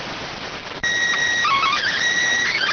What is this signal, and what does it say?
SSTV (slow-scan television), PD 290.